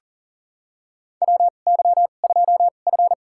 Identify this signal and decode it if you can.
Morse code, heard as a keyed tone, about 40 wpm, spaced as for 29 wpm, 690 Hz. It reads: WY2F
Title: USA Call Signs - 4 Characters - 2x Character Spacing - 40wpm